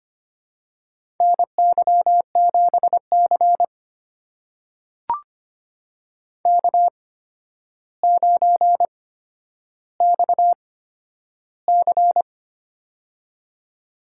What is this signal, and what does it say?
Morse code, heard as a keyed tone, about 25 wpm, spaced as for 11 wpm, 700 Hz. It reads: NY7C K9XC
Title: USA Call Signs - 4 Characters - 8x Character Spacing - 25wpm